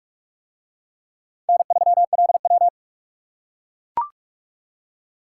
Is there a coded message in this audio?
Yes: Morse code from 1.5 to 2.7 s, heard as a keyed tone, about 44 wpm, 700 Hz. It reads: N3LW